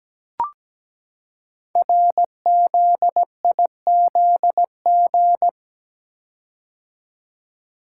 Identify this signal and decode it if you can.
Morse code, heard as a keyed tone, about 17 wpm, 700 Hz. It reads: RZIZG
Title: Code Groups - 5 Letters - 17wpm